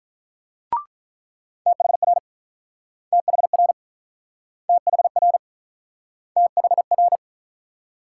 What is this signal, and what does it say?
Morse code, heard as a keyed tone, about 48 wpm, 700 Hz. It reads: THR THR THR THR